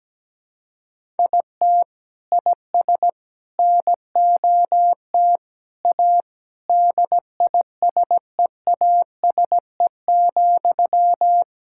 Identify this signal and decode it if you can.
Morse code, heard as a keyed tone, about 17 wpm, 700 Hz. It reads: IT IS NOT A DISEASE,
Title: Sentences from Top 1000 Words - Review Part 6 - 17wpm